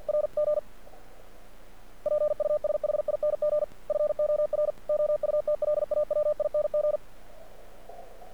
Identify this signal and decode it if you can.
Morse code, heard as a keyed tone, about 48 wpm, 600 Hz. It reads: RG PUSHING FOR OUTLAWING
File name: archived / CRSnet